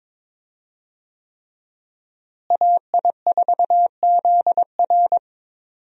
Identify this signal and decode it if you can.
Morse code, heard as a keyed tone, about 22 wpm, 700 Hz. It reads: AI4ZR
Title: USA Call Signs - 5 Characters - Repeated 3x - 22wpm